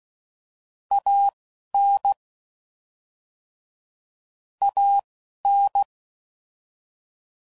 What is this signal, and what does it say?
Morse code, heard as a keyed tone, about 16 wpm, spaced as for 11 wpm, 800 Hz. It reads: AN AN
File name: Mixed / Short Runs / ShortContrasts